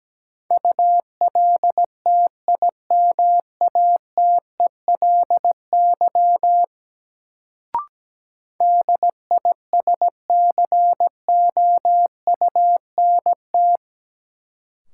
Morse code, heard as a keyed tone, about 17 wpm, 700 Hz. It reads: ULTIMATELY DISCOUNT
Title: Sentences from Top 1200 Words - New Word Drill Alt - 17wpm